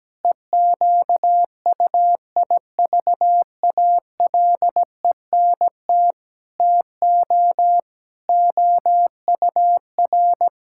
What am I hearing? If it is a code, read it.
Morse code, heard as a keyed tone, about 17 wpm, 700 Hz. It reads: EQUIVALENT TO OUR
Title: Sentences from Top 1600 Words 17wpm